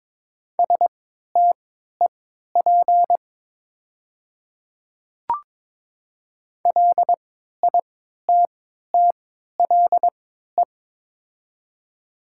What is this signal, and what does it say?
Morse code, heard as a keyed tone, about 22 wpm, spaced as for 12 wpm, 700 Hz. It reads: STEP LITTLE
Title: 3x Character Spacing - Top 500 Words - 22wpm